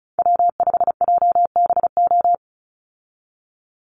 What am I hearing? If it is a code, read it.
Morse code, heard as a keyed tone, about 34 wpm, 700 Hz. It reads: W5JBO
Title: Call Signs Repeated 2x - 35wpm